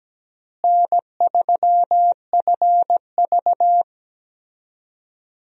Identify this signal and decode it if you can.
Morse code, heard as a keyed tone, about 17 wpm, 700 Hz. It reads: N3FV
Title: USA Call Signs - 4 Characters - Repeated 3x - 17wpm